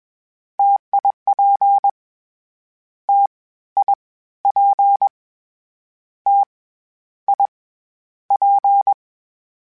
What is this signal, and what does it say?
Morse code, heard as a keyed tone, about 21 wpm, 800 Hz. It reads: TIP T I P T I P
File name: Variable Spacing / Variable Spacing-mp3s / T AOEs @20